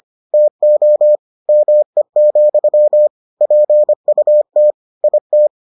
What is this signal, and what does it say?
Morse code, heard as a keyed tone, about 25 wpm, 600 Hz. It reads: TO ME, PUT IT